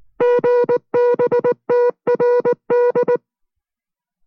Morse code, heard as a keyed tone, about 19 wpm, 490 Hz. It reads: GBTRD